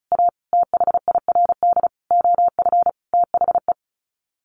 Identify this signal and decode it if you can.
Morse code, heard as a keyed tone, about 34 wpm, 700 Hz. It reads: A THIRD OF THE
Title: Sets of 5 Words Encore2 35wpm